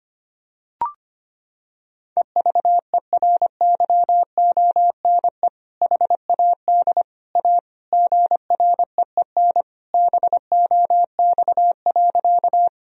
Morse code, heard as a keyed tone, about 25 wpm, 700 Hz. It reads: EVERYONE HAD A GREEN BOX.
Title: Sentences from Top 1000 Words - Review Part 2 - 25wpm